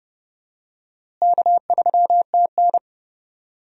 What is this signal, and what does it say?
Morse code, heard as a keyed tone, about 30 wpm, 700 Hz. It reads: K3TN
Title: USA Call Signs - 4 Characters - Repeated 3x - 30wpm